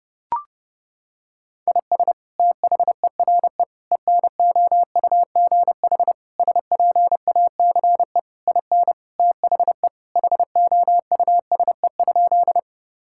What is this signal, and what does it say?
Morse code, heard as a keyed tone, about 30 wpm, 700 Hz. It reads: IS THERE ENOUGH SPACE IN THE HOUSE?
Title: Sentences from Top 1000 Words - Review Part 5 - 30wpm